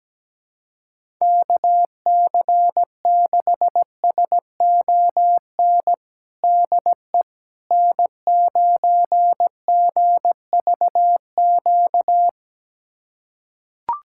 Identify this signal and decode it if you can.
Morse code, heard as a keyed tone, about 17 wpm, 700 Hz. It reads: KC6SON DE N9GVQ
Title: USA Callsign DE Callsign 17wpm